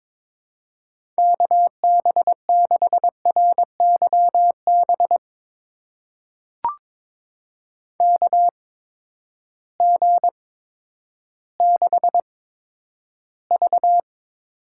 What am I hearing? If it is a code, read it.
Morse code, heard as a keyed tone, about 22 wpm, spaced as for 11 wpm, 700 Hz. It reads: KB6RYB KG6V
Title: USA Call Signs - 6 Characters - 8x Character Spacing - 22wpm